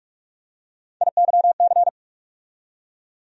Morse code, heard as a keyed tone, about 46 wpm, 700 Hz. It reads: IYC
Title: USA Call Sign Postfixes - 3 Letters 45wpm